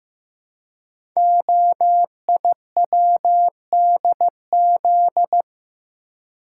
Morse code, heard as a keyed tone, about 15 wpm, 700 Hz. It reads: OIWDZ
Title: Code Groups - 5 Letters - 15wpm